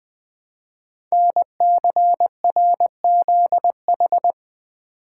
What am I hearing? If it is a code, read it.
Morse code, heard as a keyed tone, about 20 wpm, 700 Hz. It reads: NCRZH
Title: Code Groups - 5 Letters - 20wpm